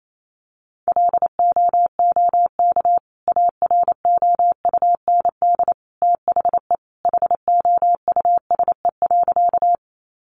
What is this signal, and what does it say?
Morse code, heard as a keyed tone, about 28 wpm, 700 Hz. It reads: LOOK AROUND THE HOUSE.
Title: Sentences from Top 600 Words 28wpm